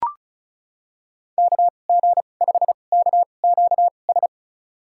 Morse code, heard as a keyed tone, about 36 wpm, spaced as for 25 wpm, 700 Hz. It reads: KG5KQS
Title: USA Call Signs - 6 Characters - 2x Character Spacing - 35wpm